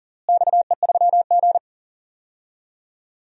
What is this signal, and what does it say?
Morse code, heard as a keyed tone, about 40 wpm, 700 Hz. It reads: XE3G
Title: DXCC-100 Call Signs - 4 Characters - Repeated 3x - 40wpm